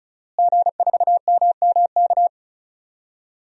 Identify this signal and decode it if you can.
Morse code, heard as a keyed tone, about 36 wpm, 700 Hz. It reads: G4MMK
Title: DXCC-100 Call Signs - 5 Characters - Repeated 3x - 35wpm